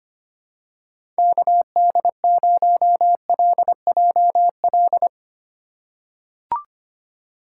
Morse code, heard as a keyed tone, about 25 wpm, 700 Hz. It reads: KD0LJL
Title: USA Call Signs - 2x Character Spacing - 25wpm